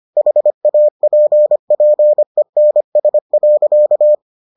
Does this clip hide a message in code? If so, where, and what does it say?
Yes: Morse code from 0.2 to 4.1 s, heard as a keyed tone, about 25 wpm, 600 Hz. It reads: HAPPENS.